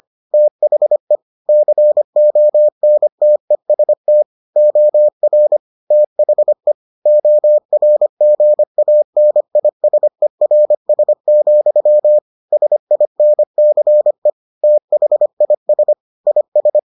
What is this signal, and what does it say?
Morse code, heard as a keyed tone, about 25 wpm, 600 Hz. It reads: THE CONTEST OR THE ORGANISERS, SINCE THIS IS